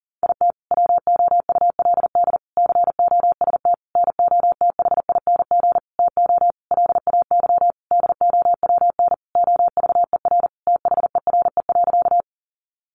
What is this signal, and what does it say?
Morse code, heard as a keyed tone, about 40 wpm, 700 Hz. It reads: IT WOULD COST NOTHING TO LAY DOWN OVER THERE.
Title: Sentences from Top 800 Words 40wpm